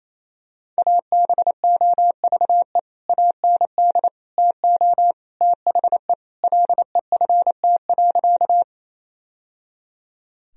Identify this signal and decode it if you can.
Morse code, heard as a keyed tone, about 28 wpm, 700 Hz. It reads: ABOVE AND TO THE LEFT.
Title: Sentences from Top 1000 Words - Review Part 2 - 28wpm